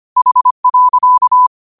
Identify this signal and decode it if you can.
Morse code, heard as a keyed tone, about 25 wpm, 1000 Hz. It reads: H.
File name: qotdfiles / 25wpm